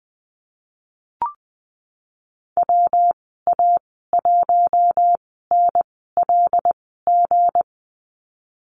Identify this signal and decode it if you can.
Morse code, heard as a keyed tone, about 20 wpm, spaced as for 15 wpm, 700 Hz. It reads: WA1NLG
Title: 167 Lesson 27g - Number 7 - Call Signs 20wpm